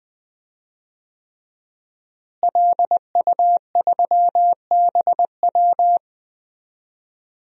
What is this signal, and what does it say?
Morse code, heard as a keyed tone, about 20 wpm, 700 Hz. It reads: LU3BW
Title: DXCC-100 Call Signs - 5 Characters - Repeated 3x - 20wpm